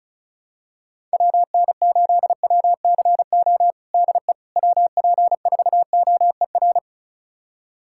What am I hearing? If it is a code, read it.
Morse code, heard as a keyed tone, about 34 wpm, 690 Hz. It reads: WN8WCO DE WP4OER
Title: USA Callsign DE Callsign - 6 Characters - 35wpm